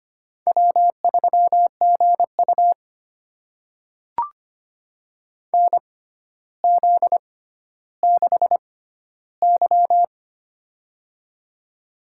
Morse code, heard as a keyed tone, about 25 wpm, spaced as for 13 wpm, 700 Hz. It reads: W3GU NZ6Y
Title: USA Call Signs - 4 Characters - 6x Character Spacing - 25wpm